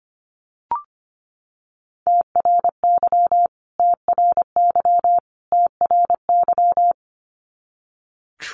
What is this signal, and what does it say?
Morse code, heard as a keyed tone, about 25 wpm, 700 Hz. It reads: TRY TRY TRY
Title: Top 200 Words Repeated 25wpm